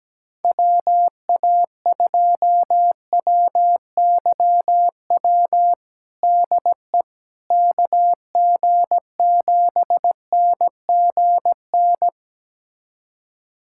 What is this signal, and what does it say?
Morse code, heard as a keyed tone, about 17 wpm, 700 Hz. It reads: WA2WYW DE KG7NGN
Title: USA Callsign DE Callsign - 6 Characters - 17wpm